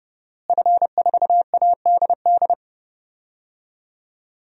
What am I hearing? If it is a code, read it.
Morse code, heard as a keyed tone, about 30 wpm, 700 Hz. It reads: F4ADD
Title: DXCC-100 Call Signs - 5 Characters - Repeated 2x - 30wpm